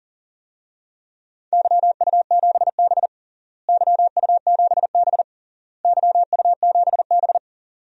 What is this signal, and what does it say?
Morse code, heard as a keyed tone, about 37 wpm, 700 Hz. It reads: YU7B YU7B YU7B
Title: DXCC-100 Call Signs - 4 Characters - Repeated 3x - 40wpm